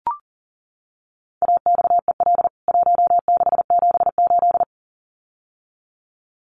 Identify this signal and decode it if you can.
Morse code, heard as a keyed tone, about 40 wpm, 700 Hz. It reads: AXEL 1678
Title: CWT Exchange 40wpm